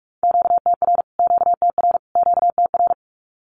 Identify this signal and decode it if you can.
Morse code, heard as a keyed tone, about 44 wpm, 700 Hz. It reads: QTR QTR QTR
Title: Q-Codes Repeated 3x 45wpm